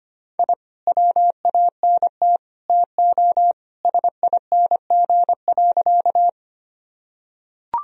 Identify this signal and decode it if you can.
Morse code, heard as a keyed tone, about 25 wpm, 700 Hz. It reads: I WANT TO SING.
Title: Sentences from Top 1000 Words - Review Part 2 - 25wpm